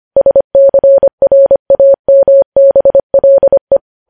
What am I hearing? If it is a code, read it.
Morse code, heard as a keyed tone, about 25 wpm, 550 Hz. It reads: SCRAMBLE
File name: podcast / News headlines-1x-25x25